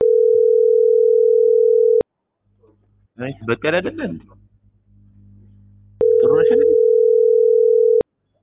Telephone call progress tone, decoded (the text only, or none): ringback tone